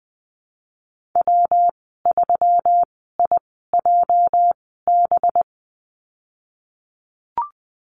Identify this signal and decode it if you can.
Morse code, heard as a keyed tone, about 19 wpm, spaced as for 14 wpm, 700 Hz. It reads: W3IJB
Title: 231 Lesson 37g - Letter x - Call Signs 20wpm